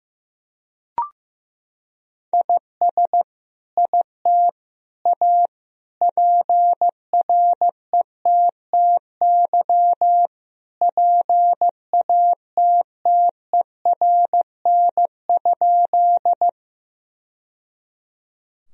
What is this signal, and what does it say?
Morse code, heard as a keyed tone, about 15 wpm, 700 Hz. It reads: IS IT A PRETTY PATTERN?
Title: Sentences from Top 1000 Words - Review Part 2 - 15wpm